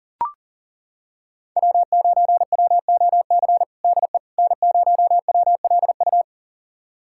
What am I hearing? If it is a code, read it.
Morse code, heard as a keyed tone, about 37 wpm, 700 Hz. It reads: W9WOC DE N0WLU
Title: USA Callsign DE Callsign - 5 Characters - 40wpm